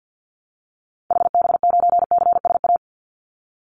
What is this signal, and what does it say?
Morse code, heard as a keyed tone, about 50 wpm, 700 Hz. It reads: HB9CSA
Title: Call Signs - 50wpm